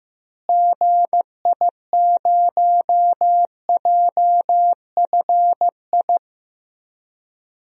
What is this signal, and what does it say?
Morse code, heard as a keyed tone, about 15 wpm, 700 Hz. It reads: GI0JFI
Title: DXCC-100 Call Signs - 6 Characters - Repeated 3x - 15wpm